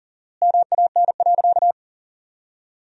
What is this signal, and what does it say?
Morse code, heard as a keyed tone, about 40 wpm, 700 Hz. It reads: MAN.
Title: 4x Word-Spacing - Sentences from Top 300 Words - 40wpm